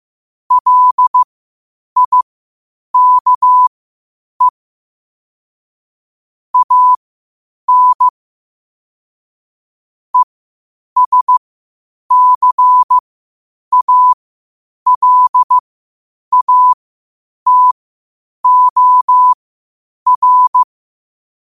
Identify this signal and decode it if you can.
Morse code, heard as a keyed tone, about 15 wpm, spaced as for 8 wpm, 1000 Hz. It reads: LIKE AN ESCALATOR
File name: qotdfiles / 8wpm